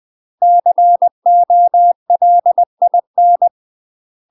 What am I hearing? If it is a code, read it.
Morse code, heard as a keyed tone, about 20 wpm, 700 Hz. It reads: COLIN
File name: UK MASTER QSOs TRAINING/20WPM